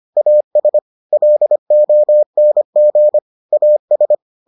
Morse code, heard as a keyed tone, about 25 wpm, 600 Hz. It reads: AS LONG AS